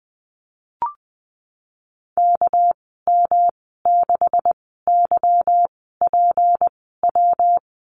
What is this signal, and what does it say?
Morse code, heard as a keyed tone, about 20 wpm, spaced as for 15 wpm, 700 Hz. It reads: KM6YPW
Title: 200 Lesson 32g - Letter k - Call Signs 20wpm